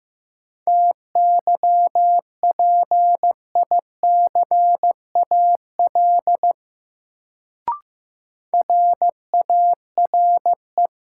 Morse code, heard as a keyed tone, about 15 wpm, 700 Hz. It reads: TYPICAL RARE
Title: Sentences from Top 1100 Words - New Word Drill - 15wpm